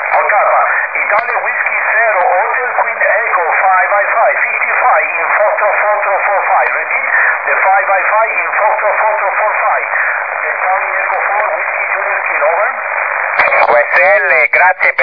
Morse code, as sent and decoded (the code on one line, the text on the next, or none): none